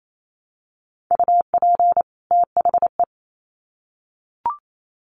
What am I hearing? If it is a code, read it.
Morse code, heard as a keyed tone, about 28 wpm, 700 Hz. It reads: UP THE
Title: Sets of 2 Words 28wpm